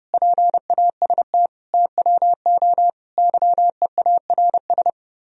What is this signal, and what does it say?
Morse code, heard as a keyed tone, about 38 wpm, 700 Hz. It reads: PAST TWO YEARS